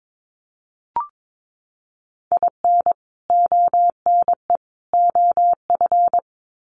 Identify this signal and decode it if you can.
Morse code, heard as a keyed tone, about 22 wpm, 700 Hz. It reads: IN ONE OF
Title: Sets of 3 Words 22wpm